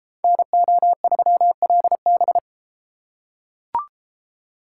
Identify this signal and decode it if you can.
Morse code, heard as a keyed tone, about 33 wpm, 700 Hz. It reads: NO3LB